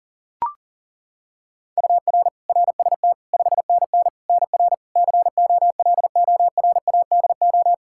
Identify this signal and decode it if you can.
Morse code, heard as a keyed tone, about 37 wpm, 700 Hz. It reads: UR RST 5NN NR COLORADO
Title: Practice QSO - Exchange 1 - Beginning - 40wpm